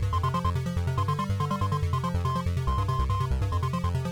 Morse code, heard as a keyed tone, about 22 wpm, 1050 Hz, with noise. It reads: H SHIT OH
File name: NewJayTracks / Specific Games / Morse4